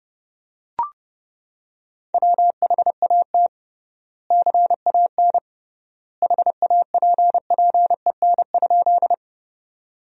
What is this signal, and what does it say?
Morse code, heard as a keyed tone, about 30 wpm, 700 Hz. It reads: WHAT CAN HAPPEN?
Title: 3x Word-Spacing - Sentences from Top 400 Words - 30wpm